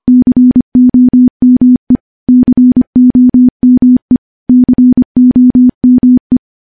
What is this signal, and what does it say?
Morse code, heard as a keyed tone, about 25 wpm, 250 Hz. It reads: COME COME COME
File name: morse-training-25WPM